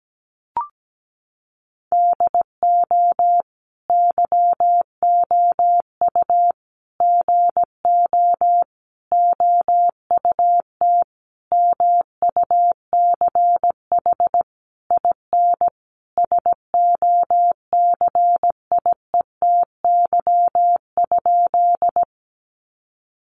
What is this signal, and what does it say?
Morse code, heard as a keyed tone, about 17 wpm, 700 Hz. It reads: DO YOU GO OUT MUCH IN SOCIETY?
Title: Sentences from Top 700 Words 17wpm